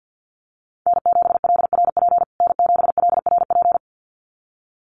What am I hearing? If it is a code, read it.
Morse code, heard as a keyed tone, about 50 wpm, 700 Hz. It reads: N7LRP N7LRP
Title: USA Call Signs Repeated 2x 50wpm